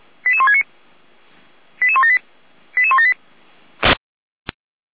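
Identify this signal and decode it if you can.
Five-tone selective call (ZVEI), heard as quick runs of five tones, tones of about 70 ms. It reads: 88169 88178 88178
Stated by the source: ZVEI five-tone call received by SDR.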